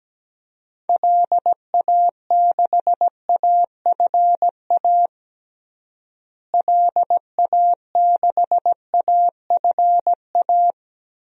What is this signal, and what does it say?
Morse code, heard as a keyed tone, about 17 wpm, 700 Hz. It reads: LA6AFA LA6AFA
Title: DXCC-100 Call Signs - 6 Characters - Repeated 2x - 17wpm